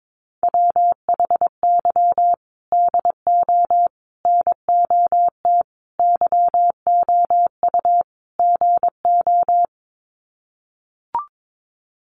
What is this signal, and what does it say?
Morse code, heard as a keyed tone, about 22 wpm, 700 Hz. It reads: WHY DO NOT YOU GO
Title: Sets of 5 Words Encore 22wpm